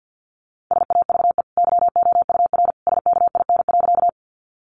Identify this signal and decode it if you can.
Morse code, heard as a keyed tone, about 48 wpm, 700 Hz. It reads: SAVE YOUR SKIN.
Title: Sentences from Top 1300 Words 50wpm